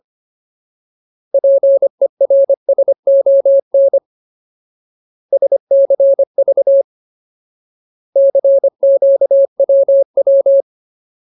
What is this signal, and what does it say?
Morse code, heard as a keyed tone, about 25 wpm, 550 Hz. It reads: PERSON SCV CQWW
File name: Ultimate Word List / 25WPM Ultimate